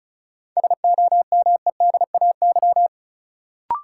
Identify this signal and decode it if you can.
Morse code, heard as a keyed tone, about 36 wpm, 700 Hz. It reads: SOMEDAY
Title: Sentences from Top 1600 Words - New Word Drill Alt - 35wpm